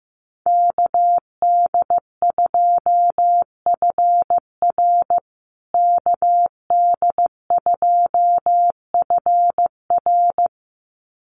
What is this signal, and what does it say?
Morse code, heard as a keyed tone, about 15 wpm, 700 Hz. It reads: KD2FR KD2FR
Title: USA Call Signs Repeated 2x 15wpm